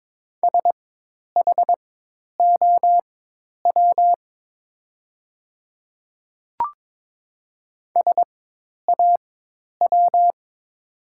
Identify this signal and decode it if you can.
Morse code, heard as a keyed tone, about 22 wpm, spaced as for 10 wpm, 700 Hz. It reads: SHOW SAW
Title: 4x Character Spacing - Top 500 Words - 22wpm